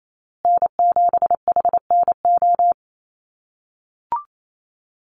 Morse code, heard as a keyed tone, about 28 wpm, 700 Hz. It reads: N7HNO
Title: USA Call Signs Repeated 2x 28wpm